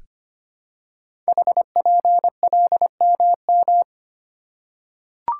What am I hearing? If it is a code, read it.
Morse code, heard as a keyed tone, about 25 wpm, 700 Hz. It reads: HPLMM
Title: Code Groups - 5 Letters - 25wpm